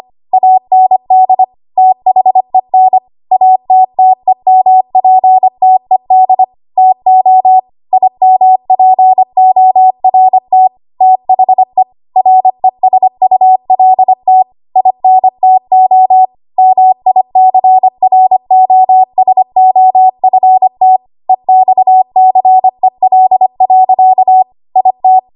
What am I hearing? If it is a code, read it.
Morse code, heard as a keyed tone, about 25 wpm, 750 Hz. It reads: AND THEN ATTEMPTED TO IMPORT THE RESULT INTO MICROSOFT EXCEL. IT
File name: Archive / 25 WPM / 25WPM